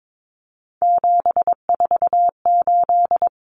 Morse code, heard as a keyed tone, about 22 wpm, 700 Hz. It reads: 748
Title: Common Cut Numbers - 3-Digit 22wpm